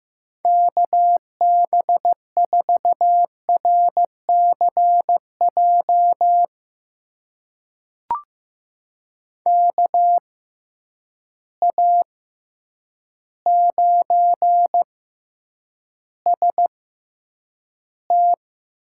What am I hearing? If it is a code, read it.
Morse code, heard as a keyed tone, about 15 wpm, spaced as for 8 wpm, 700 Hz. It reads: KB4RCJ KA9ST
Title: USA Call Signs - 6 Characters - 6x Character Spacing - 15wpm